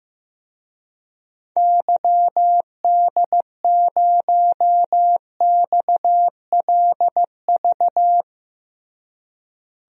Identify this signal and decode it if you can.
Morse code, heard as a keyed tone, about 15 wpm, 700 Hz. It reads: YD0XLV
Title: DXCC-100 Call Signs - 6 Characters - Repeated 3x - 15wpm